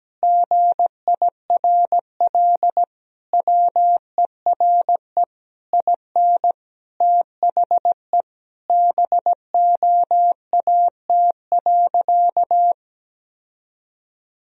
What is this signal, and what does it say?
Morse code, heard as a keyed tone, about 17 wpm, 700 Hz. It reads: GIRL WERE IN THE BOAT.
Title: Sentences from Top 1000 Words - Review Part 3 - 17wpm